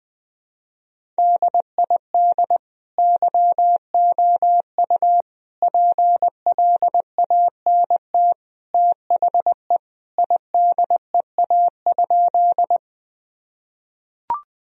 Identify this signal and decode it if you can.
Morse code, heard as a keyed tone, about 20 wpm, 700 Hz. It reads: DID YOU PLANT THE IDEA?
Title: Sentences from Top 1000 Words - Review Part 1 - 20wpm